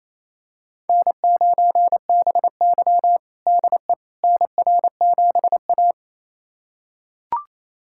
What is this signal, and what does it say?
Morse code, heard as a keyed tone, about 28 wpm, 700 Hz. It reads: N9BY DE NR7A
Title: USA Callsign DE Callsign - 4 Characters - 28wpm